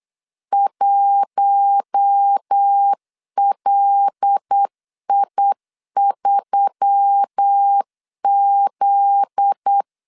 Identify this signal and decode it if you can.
Morse code, heard as a keyed tone, about 8 wpm, 800 Hz. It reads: 1LI3Z
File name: LEZIONI / 7 GF9Z / 7 GF9Z 8wpm 3213